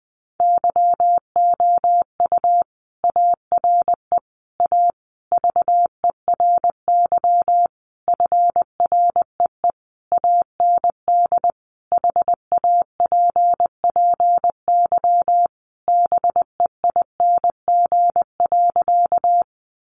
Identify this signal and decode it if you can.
Morse code, heard as a keyed tone, about 20 wpm, 700 Hz. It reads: YOU ARE A VERY FREE AND HAPPY BEING.
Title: Sentences from Top 1000 Words 20wpm